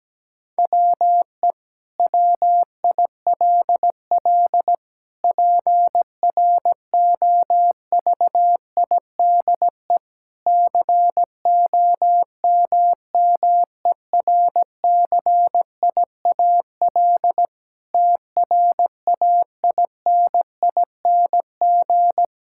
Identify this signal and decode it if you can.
Morse code, heard as a keyed tone, about 17 wpm, 700 Hz. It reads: WE WILL PROVIDE COMMERCIAL TRAINING
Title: Sentences from Top 1000 Words - Review Part 5 - 17wpm